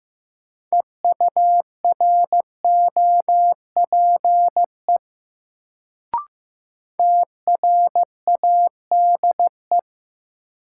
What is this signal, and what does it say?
Morse code, heard as a keyed tone, about 15 wpm, 700 Hz. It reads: EUROPE TRADE
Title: Sets of 5 Words Encore - New Word Drill - 15wpm